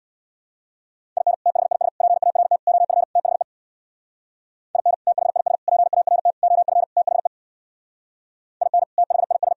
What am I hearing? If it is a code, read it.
Morse code, heard as a keyed tone, about 61 wpm, 700 Hz. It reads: IN THIS PART OF THE IN THIS PART OF THE IN THIS
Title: Speed-Racing - Sets of 5 Words 50wpm